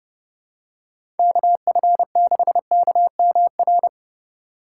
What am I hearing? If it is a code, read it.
Morse code, heard as a keyed tone, about 30 wpm, 700 Hz. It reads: KF6KMR
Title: USA Call Signs - 6 Characters - 4x Character Spacing - 30wpm